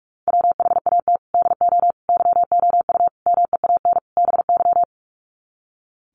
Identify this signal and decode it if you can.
Morse code, heard as a keyed tone, about 44 wpm, 700 Hz. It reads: WHAT DO YOU MEAN BY
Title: Sets of 5 Words 45wpm